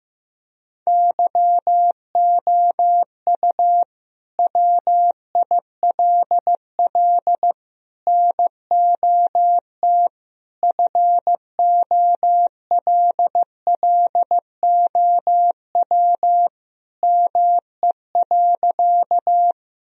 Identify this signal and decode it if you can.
Morse code, heard as a keyed tone, about 15 wpm, 700 Hz. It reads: YOU WILL NOT FOLLOW ME.
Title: Sentences from Top 1000 Words - Review Part 1 - 15wpm